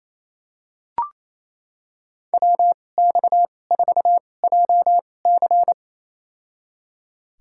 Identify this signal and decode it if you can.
Morse code, heard as a keyed tone, about 28 wpm, spaced as for 20 wpm, 700 Hz. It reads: WX4JC
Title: USA Call Signs - 5 Characters - 2x Character Spacing - 28wpm